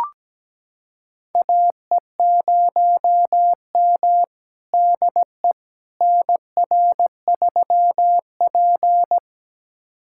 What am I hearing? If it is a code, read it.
Morse code, heard as a keyed tone, about 17 wpm, 700 Hz. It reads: AE0M DE NR3P
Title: USA Callsign DE Callsign - 4 Characters - 17wpm